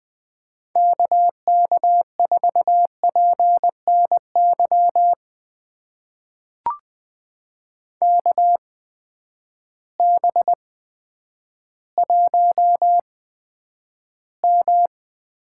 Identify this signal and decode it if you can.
Morse code, heard as a keyed tone, about 20 wpm, spaced as for 10 wpm, 700 Hz. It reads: KK4PNY KB1M
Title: USA Call Signs - 6 Characters - 8x Character Spacing - 20wpm